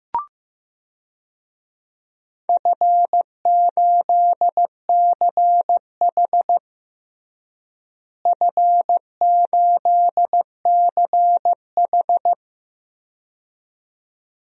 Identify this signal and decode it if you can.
Morse code, heard as a keyed tone, about 15 wpm, 700 Hz. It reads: F8CH F8CH
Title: DXCC-100 Call Signs - 4 Characters - Repeated 2x - 15wpm